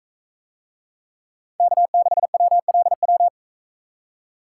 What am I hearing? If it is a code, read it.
Morse code, heard as a keyed tone, about 44 wpm, 700 Hz. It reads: K6WLW